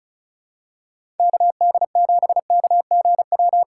Morse code, heard as a keyed tone, about 36 wpm, 700 Hz. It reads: KD7KGW
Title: USA Call Signs - 6 Characters - 3x Character Spacing - 35wpm